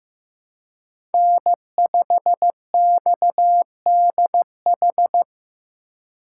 Morse code, heard as a keyed tone, about 18 wpm, 700 Hz. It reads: N5XDH
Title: Speed Racing - USA 1x3 Call Signs 15wpm